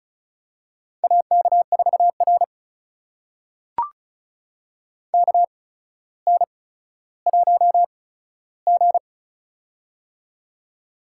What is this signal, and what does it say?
Morse code, heard as a keyed tone, about 36 wpm, spaced as for 15 wpm, 700 Hz. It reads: AK4R KN1G
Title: USA Call Signs - 4 Characters - 8x Character Spacing - 35wpm